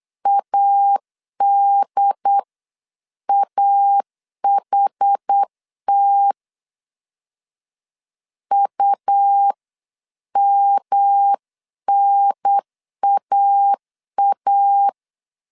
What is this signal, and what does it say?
Morse code, heard as a keyed tone, about 8 wpm, 800 Hz. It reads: ADAHT UMNAA